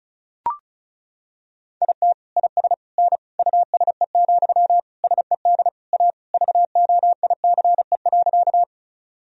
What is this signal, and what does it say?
Morse code, heard as a keyed tone, about 36 wpm, 700 Hz. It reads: IT IS N USE, SED A VOICE.
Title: Sentences from Top 500 Words and Abbreviations 35wpm